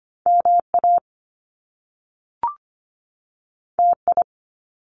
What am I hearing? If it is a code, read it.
Morse code, heard as a keyed tone, about 25 wpm, 700 Hz. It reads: MA TI
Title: Top Two Letter Combinations in Words 25wpm